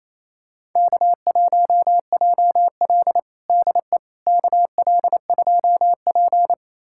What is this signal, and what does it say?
Morse code, heard as a keyed tone, about 28 wpm, 700 Hz. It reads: K1JL DE KL2P
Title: USA Callsign DE Callsign - 4 Characters - 28wpm